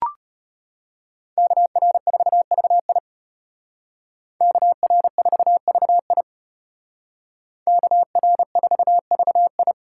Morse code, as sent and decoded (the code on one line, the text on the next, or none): -.- .-. ....- ...- .. / -.- .-. ....- ...- .. / -.- .-. ....- ...- ..
KR4VI KR4VI KR4VI